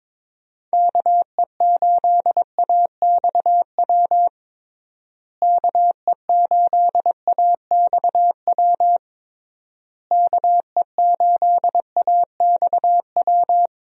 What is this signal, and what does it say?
Morse code, heard as a keyed tone, about 22 wpm, 700 Hz. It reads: KE8AXW KE8AXW KE8AXW
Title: USA Call Signs - 6 Characters - Repeated 3x - 22wpm